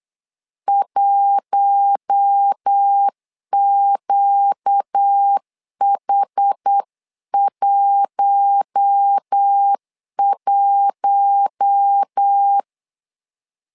Morse code, heard as a keyed tone, about 8 wpm, 800 Hz. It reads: 1QH11
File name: LEZIONI / 4 Q1 8wpm 3035